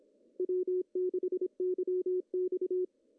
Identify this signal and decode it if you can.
Morse code, heard as a keyed tone, about 25 wpm, 340 Hz. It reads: W6YX
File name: CW / cqww-cw-k5tr